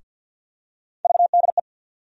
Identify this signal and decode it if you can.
Morse code, heard as a keyed tone, about 46 wpm, 700 Hz. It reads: VDE
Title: USA Call Sign Postfixes - 3 Letters 50wpm